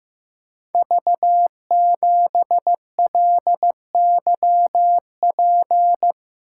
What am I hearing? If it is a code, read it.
Morse code, heard as a keyed tone, about 15 wpm, 700 Hz. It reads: V7LYP
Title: Code Groups - 5 Letter:Numbers - 15wpm